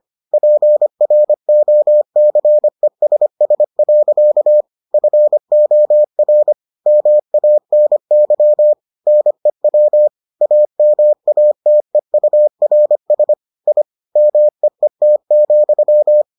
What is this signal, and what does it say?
Morse code, heard as a keyed tone, about 25 wpm, 600 Hz. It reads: PROCESS. FOR MANY NEW AMATEURS I MEET,